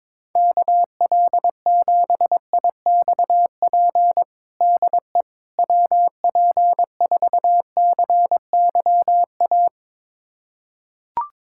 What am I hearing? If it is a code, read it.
Morse code, heard as a keyed tone, about 22 wpm, 700 Hz. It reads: KL7IXP DE WP4CYA
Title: USA Callsign DE Callsign - 6 Characters - 22wpm